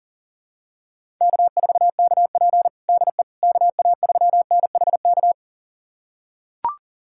Morse code, heard as a keyed tone, about 40 wpm, 700 Hz. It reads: K4KP DE KA3NHK
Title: USA Callsign DE Callsign 40wpm